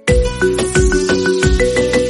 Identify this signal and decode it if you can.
There is no code.